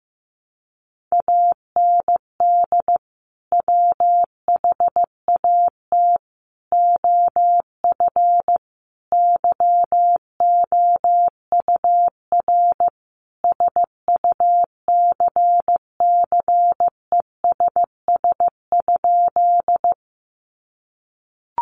Morse code, heard as a keyed tone, about 15 wpm, 700 Hz. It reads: AND WHAT OF YOUR SUCCESS?
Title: Sentences from Top 900 Words 15wpm